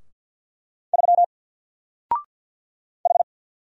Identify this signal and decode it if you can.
Morse code, heard as a keyed tone, about 48 wpm, 700 Hz. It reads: 3 H
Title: Single Character (Letter:Number:Punctuation:Prosign) 50wpm